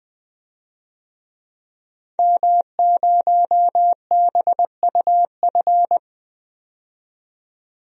Morse code, heard as a keyed tone, about 20 wpm, 700 Hz. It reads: M0BUF
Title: DXCC-100 Call Signs - 5 Characters - 20wpm